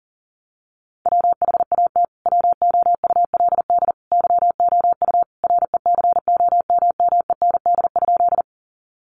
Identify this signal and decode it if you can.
Morse code, heard as a keyed tone, about 40 wpm, 700 Hz. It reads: WHAT WOULD YOU RECOMMEND?
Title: Sentences from Top 1200 Words 40wpm